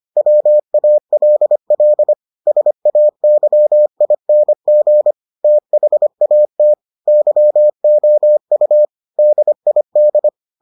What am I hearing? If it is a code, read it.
Morse code, heard as a keyed tone, about 25 wpm, 600 Hz. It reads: WALL SAYING THAT YOU DID